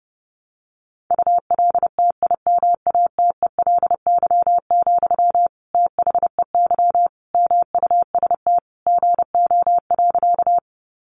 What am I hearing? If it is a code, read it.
Morse code, heard as a keyed tone, about 30 wpm, 700 Hz. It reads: ULTIMATELY, THEY MUST GO.
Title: Sentences from Top 1200 Words 30wpm